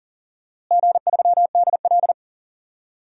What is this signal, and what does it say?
Morse code, heard as a keyed tone, about 39 wpm, 700 Hz. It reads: G3DL
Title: DXCC-100 Call Signs - 4 Characters - Repeated 3x - 40wpm